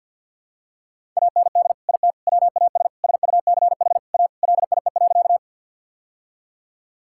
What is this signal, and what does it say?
Morse code, heard as a keyed tone, about 50 wpm, 700 Hz. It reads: AND IT WAS SUCH A LIE.
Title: Sentences from Top 1600 Words 50wpm